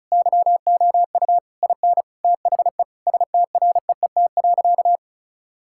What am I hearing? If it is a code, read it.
Morse code, heard as a keyed tone, about 34 wpm, 690 Hz. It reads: YOU IN THE STREET.
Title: Sentences from Top 1000 Words - Review Part 6 - 35wpm